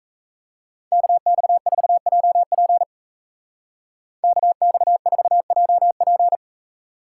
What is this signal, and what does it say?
Morse code, heard as a keyed tone, about 33 wpm, 700 Hz. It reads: KX4JP KX4JP
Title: Speed Racing - USA Call Signs 28wpm